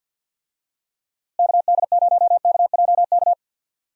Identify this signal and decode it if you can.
Morse code, heard as a keyed tone, about 52 wpm, 700 Hz. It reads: KD0KJK
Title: USA Call Signs - 6 Characters - Repeated 3x - 50wpm